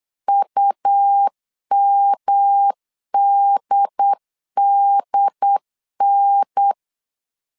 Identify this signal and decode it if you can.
Morse code, heard as a keyed tone, about 8 wpm, 800 Hz. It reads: UMDDN
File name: CW / DET / 1 3 1500C S B 8wpm 4141